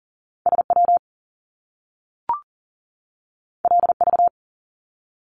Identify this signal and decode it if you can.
Morse code, heard as a keyed tone, about 39 wpm, 700 Hz. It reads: SW LV
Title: Top Two Letter Combinations in Words 40wpm